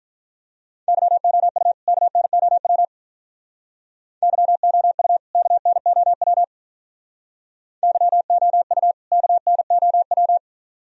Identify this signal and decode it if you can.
Morse code, heard as a keyed tone, about 44 wpm, 700 Hz. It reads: YOU KNOW YOU KNOW YOU KNOW
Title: Speed-Racing - Sets of 2 Words 35wpm